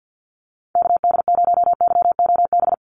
Morse code, heard as a keyed tone, about 48 wpm, 700 Hz. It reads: KD0YOB
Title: NA SOTA Activator Callsigns 50wpm